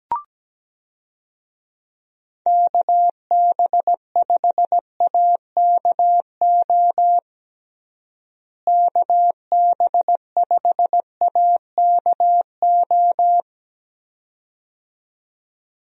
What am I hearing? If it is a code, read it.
Morse code, heard as a keyed tone, about 17 wpm, 700 Hz. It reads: KB5AKO KB5AKO
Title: USA Call Signs - 6 Characters - Repeated 2x - 17wpm